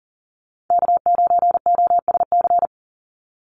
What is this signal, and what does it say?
Morse code, heard as a keyed tone, about 40 wpm, 700 Hz. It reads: K9OSC
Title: NA SOTA Chaser Callsigns 40wpm